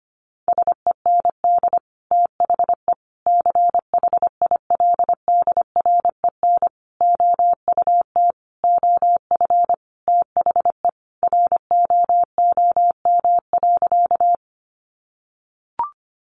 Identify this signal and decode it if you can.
Morse code, heard as a keyed tone, about 25 wpm, 700 Hz. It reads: SEND THE CHILDREN OUT OF THE ROOM.
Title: Sentences from Top 1500 Words 25wpm